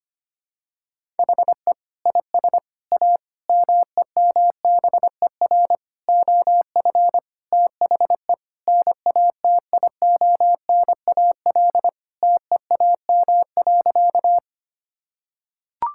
Morse code, heard as a keyed tone, about 25 wpm, 700 Hz. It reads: HE IS A MEMBER OF THE NATIONAL TEAM.
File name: practice / 25wpm / Sentences with 3-Word Phrases - 25wpm